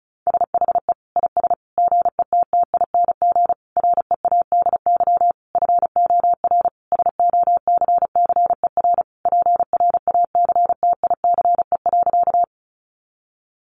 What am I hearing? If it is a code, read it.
Morse code, heard as a keyed tone, about 34 wpm, 700 Hz. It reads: SHE IS GETTING READY FOR SOCCER PRACTICE.
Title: Easy Phrases 35wpm